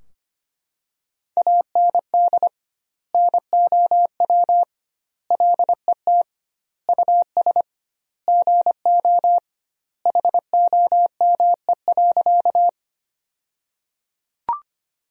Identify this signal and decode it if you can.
Morse code, heard as a keyed tone, about 25 wpm, 700 Hz. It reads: AND NOW LET US GO HOME.
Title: 2x Word-Spacing - Sentences from Top 300 Words - 25wpm